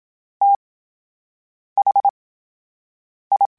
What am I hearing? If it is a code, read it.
Morse code, heard as a keyed tone, about 27 wpm, spaced as for 7 wpm, 800 Hz. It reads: THI
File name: Special INGs / Special INGs-mp3s / VerySlowToFaster Ing @25